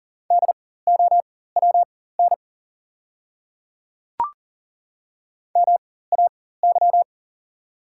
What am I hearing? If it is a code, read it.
Morse code, heard as a keyed tone, about 40 wpm, spaced as for 19 wpm, 690 Hz. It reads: DOWN MAY
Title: 4x Character Spacing - Top 200 Words - 40wpm